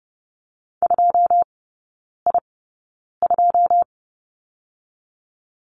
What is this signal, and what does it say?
Morse code, heard as a keyed tone, about 30 wpm, spaced as for 9 wpm, 700 Hz. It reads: 2I2
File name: course / 30:10wpm / 087 Lesson 15e - Number 5 - Three Characters 30wpm